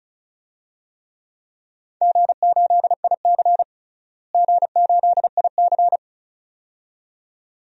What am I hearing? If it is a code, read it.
Morse code, heard as a keyed tone, about 34 wpm, 700 Hz. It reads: G8IC G8IC
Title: DXCC-100 Call Signs - 4 Characters - Repeated 2x - 35wpm